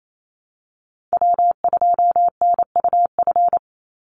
Morse code, heard as a keyed tone, about 28 wpm, 700 Hz. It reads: W2NUF
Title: USA Call Signs 28wpm